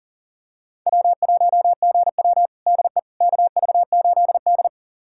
Morse code, heard as a keyed tone, about 40 wpm, 700 Hz. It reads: W1GW DE KV8D
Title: USA Callsign DE Callsign - 4 Characters - 40wpm